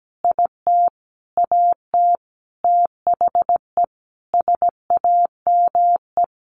Morse code, heard as a keyed tone, about 17 wpm, 700 Hz. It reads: IT AT THE SAME
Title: Sentences from Top 1000 Words 17wpm